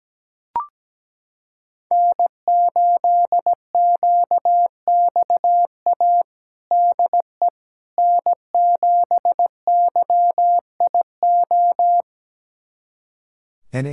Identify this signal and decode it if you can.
Morse code, heard as a keyed tone, about 17 wpm, 700 Hz. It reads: N8QXA DE N7YIO
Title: USA Callsign DE Callsign - 5 Characters - 17wpm